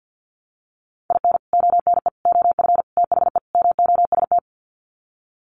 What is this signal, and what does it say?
Morse code, heard as a keyed tone, about 48 wpm, 700 Hz. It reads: IN ONE OF THE MOST
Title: Sets of 5 Words Encore 50wpm